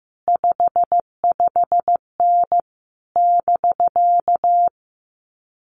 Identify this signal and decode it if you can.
Morse code, heard as a keyed tone, about 15 wpm, 700 Hz. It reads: 55N <BK>
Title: Common QSO Phrases Encore 15wpm